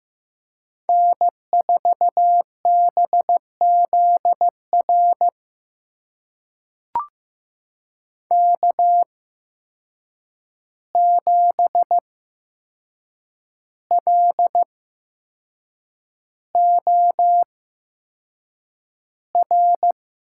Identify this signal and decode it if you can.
Morse code, heard as a keyed tone, about 15 wpm, spaced as for 6 wpm, 700 Hz. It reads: N4BZR K7LOR
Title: USA Call Signs - 5 Characters - 8x Character Spacing - 15wpm